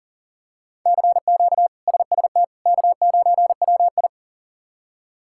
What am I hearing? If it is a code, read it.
Morse code, heard as a keyed tone, about 39 wpm, 700 Hz. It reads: CQ SST K9WI
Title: K1USN SST QSO Practice - 40wpm